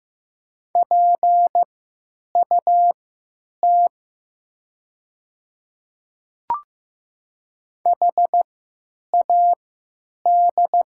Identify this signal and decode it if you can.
Morse code, heard as a keyed tone, about 15 wpm, spaced as for 8 wpm, 700 Hz. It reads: PUT HAD
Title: 3x Character Spacing - Top 500 Words - 15wpm